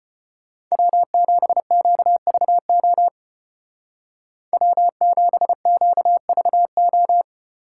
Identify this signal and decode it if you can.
Morse code, heard as a keyed tone, about 32 wpm, 700 Hz. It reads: W7QVO W7QVO